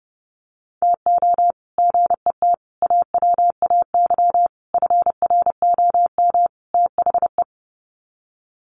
Morse code, heard as a keyed tone, about 28 wpm, 700 Hz. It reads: TO GET AWAY FROM THE
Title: Sets of 5 Words Encore2 30wpm